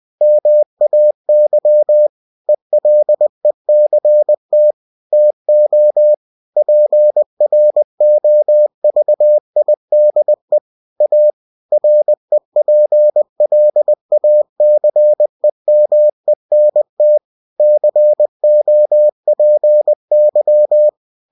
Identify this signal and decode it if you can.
Morse code, heard as a keyed tone, about 20 wpm, 600 Hz. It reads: MAY ELECT TO PROVIDE A REPLACEMENT COPY
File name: cw-books / winnie-the-pooh / winnie-20wpm-page-0119